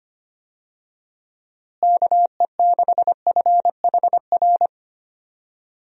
Morse code, heard as a keyed tone, about 25 wpm, 700 Hz. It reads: KE6FHR
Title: USA Call Signs - 6 Characters - Repeated 2x - 25wpm